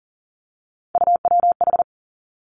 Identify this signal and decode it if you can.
Morse code, heard as a keyed tone, about 38 wpm, 700 Hz. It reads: UWH
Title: Three Challenging Letter-Number Combinations 40wpm